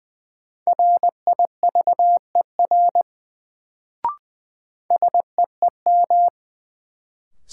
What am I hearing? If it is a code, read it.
Morse code, heard as a keyed tone, about 20 wpm, 700 Hz. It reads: RIVER SEEM
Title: Sentences from Top 300 Words - New Word Drill - 20wpm